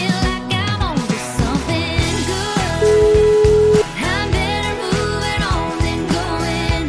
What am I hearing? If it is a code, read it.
Ringback tone (Europe, 425 Hz), with noise.